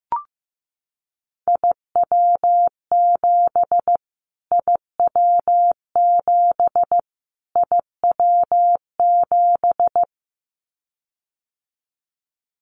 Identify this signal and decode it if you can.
Morse code, heard as a keyed tone, about 15 wpm, 700 Hz. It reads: IW7 IW7 IW7
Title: Callsign Prefixes for Top 25 DXCC - Repeated 3x - 15wpm